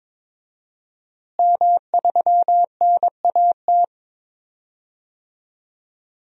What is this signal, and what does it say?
Morse code, heard as a keyed tone, about 22 wpm, 700 Hz. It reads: M3NAT
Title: DXCC-100 Call Signs - 5 Characters - Repeated 2x - 22wpm